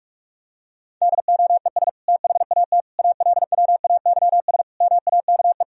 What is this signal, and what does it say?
Morse code, heard as a keyed tone, about 45 wpm, 700 Hz. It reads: DOES THAT ALWAYS MAKE
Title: Sentences from Top 1000 Words - Review Part 2 - 45wpm